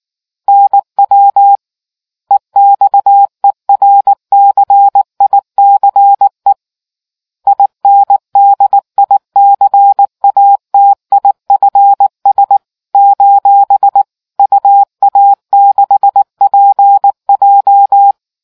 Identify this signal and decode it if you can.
Morse code, heard as a keyed tone, about 19 wpm, 800 Hz. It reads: NW EXERCICE INDICATIFS : UA6PJ